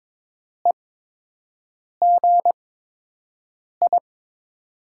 Morse code, heard as a keyed tone, about 22 wpm, spaced as for 6 wpm, 700 Hz. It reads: EGI
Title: USA Call Signs - 5 Characters - 8x Character Spacing - 22wpm